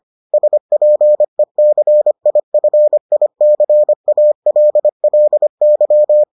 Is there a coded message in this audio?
Yes: Morse code from 0.3 to 6.2 s, heard as a keyed tone, about 25 wpm, 600 Hz. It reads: SPECIFICALLY